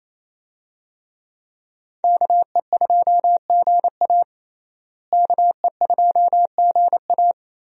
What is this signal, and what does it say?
Morse code, heard as a keyed tone, about 27 wpm, 700 Hz. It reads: KE2GA KE2GA
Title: USA Call Signs - 5 Characters - Repeated 3x - 28wpm